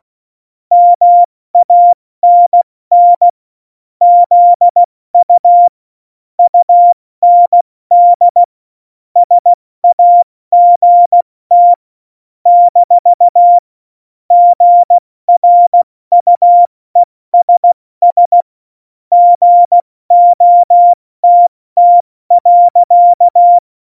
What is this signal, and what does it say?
Morse code, heard as a keyed tone, about 16 wpm, 700 Hz. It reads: MANN ZU UND SAGT - GRUESS GOTT.
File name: Morsekurs / Witze / Witz 07. November 2017, 70 F80